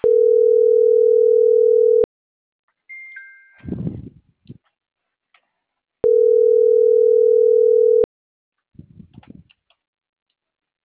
Ringback tone (North America), 440 + 480 Hz.